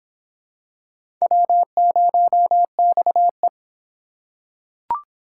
Morse code, heard as a keyed tone, about 22 wpm, 700 Hz. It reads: W0XE